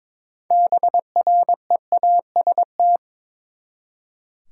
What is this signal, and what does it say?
Morse code, heard as a keyed tone, about 22 wpm, 700 Hz. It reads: BREAST
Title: Six Letter Words - 22wpm